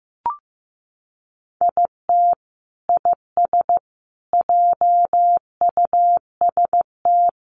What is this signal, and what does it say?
Morse code, heard as a keyed tone, about 15 wpm, 700 Hz. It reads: IT IS JUST
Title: Sentences from Top 600 Words 15wpm